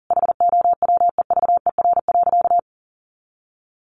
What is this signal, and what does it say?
Morse code, heard as a keyed tone, about 40 wpm, 700 Hz. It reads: HOWEVER.
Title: Sentences from Top 600 Words 40wpm